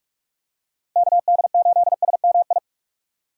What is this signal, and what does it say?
Morse code, heard as a keyed tone, about 43 wpm, 700 Hz. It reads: KD8SMI
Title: USA Call Signs - 6 Characters - 4x Character Spacing - 45wpm